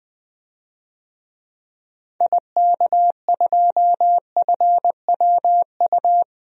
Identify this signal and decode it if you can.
Morse code, heard as a keyed tone, about 20 wpm, 700 Hz. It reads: IK2FWU
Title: DXCC-100 Call Signs - 6 Characters - 20wpm